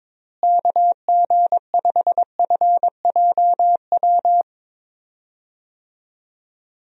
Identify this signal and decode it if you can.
Morse code, heard as a keyed tone, about 22 wpm, 700 Hz. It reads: KG5FJW
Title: USA Call Signs - 6 Characters - Repeated 2x - 22wpm